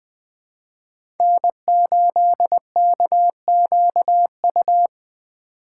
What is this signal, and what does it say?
Morse code, heard as a keyed tone, about 20 wpm, 700 Hz. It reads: N8KQU
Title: USA Call Signs - 5 Characters - 4x Character Spacing - 20wpm